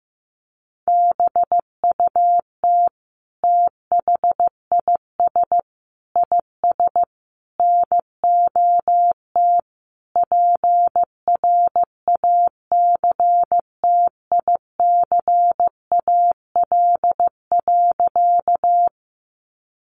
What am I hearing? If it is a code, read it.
Morse code, heard as a keyed tone, about 15 wpm, 700 Hz. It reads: BUT THIS IS NOT PRACTICAL.
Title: Sentences from Top 1400 Words 15wpm